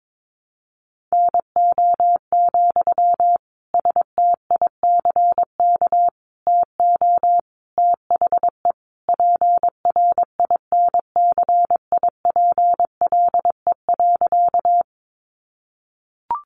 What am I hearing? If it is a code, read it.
Morse code, heard as a keyed tone, about 22 wpm, 700 Hz. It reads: NO, STICK TO THE PRINCIPLE.
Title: Sentences from Top 1400 Words 22wpm